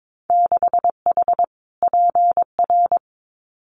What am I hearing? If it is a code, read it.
Morse code, heard as a keyed tone, about 22 wpm, 700 Hz. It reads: 6H PR
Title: Winter Field Day Exchanges Encore 22wpm